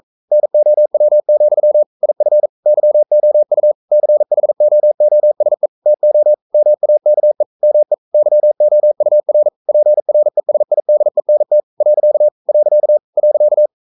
Morse code, heard as a keyed tone, about 42 wpm, 600 Hz. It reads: NOW, IF YOU CHOOSE TO MAKE ME YOUR PRESIDENT . .
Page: Romanen Never af Ken Follet på engelsk.